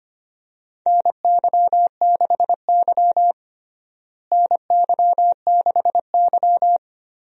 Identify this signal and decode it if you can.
Morse code, heard as a keyed tone, about 25 wpm, 700 Hz. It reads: NY6Y NY6Y
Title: USA Call Signs - 4 Characters - Repeated 3x - 25wpm